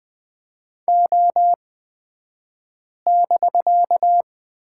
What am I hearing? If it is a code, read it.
Morse code, heard as a keyed tone, about 20 wpm, spaced as for 5 wpm, 700 Hz. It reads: O<BK>
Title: 249 Lesson 40d - BK - Two Characters 20wpm